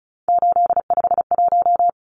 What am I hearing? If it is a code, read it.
Morse code, heard as a keyed tone, about 34 wpm, 700 Hz. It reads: 851
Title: Numbers - 3-Digit 35wpm